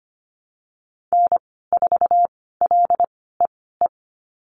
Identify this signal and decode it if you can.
Morse code, heard as a keyed tone, about 25 wpm, spaced as for 16 wpm, 700 Hz. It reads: N4LEE
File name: Lesson 15g - Number 5 - Call Signs 25wpm